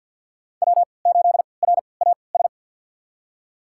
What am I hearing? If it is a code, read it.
Morse code, heard as a keyed tone, about 46 wpm, spaced as for 27 wpm, 700 Hz. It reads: W8RAS
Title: USA Call Signs - 5 Characters - 3x Character Spacing - 50wpm